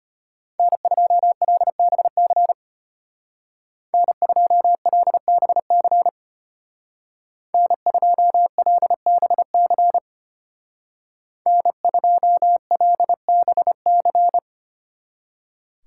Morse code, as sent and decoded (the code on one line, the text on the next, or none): -. ..--- .-.. -... -.-. / -. ..--- .-.. -... -.-. / -. ..--- .-.. -... -.-. / -. ..--- .-.. -... -.-.
N2LBC N2LBC N2LBC N2LBC